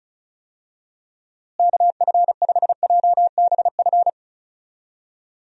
Morse code, heard as a keyed tone, about 36 wpm, 700 Hz. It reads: KF5JBF
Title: USA Call Signs - 6 Characters - Repeated 2x - 35wpm